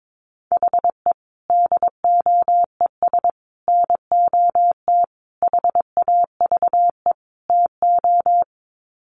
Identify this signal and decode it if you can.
Morse code, heard as a keyed tone, about 22 wpm, 700 Hz. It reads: HE DOES NOT HAVE TO
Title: Sets of 5 Words Encore 22wpm